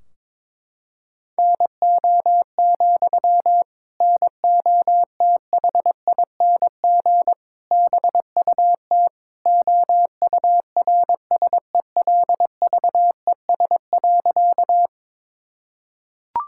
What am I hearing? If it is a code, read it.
Morse code, heard as a keyed tone, about 22 wpm, 700 Hz. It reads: NO, NOTHING BUT OURSELVES.
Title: Sentences from Top 1600 Words 22wpm